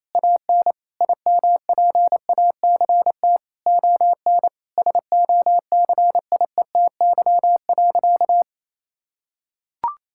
Morse code, heard as a keyed tone, about 28 wpm, 700 Hz. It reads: AN IMPACT ON SOCIETY.
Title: Sentences from Top 1000 Words - Review Part 6 - 28wpm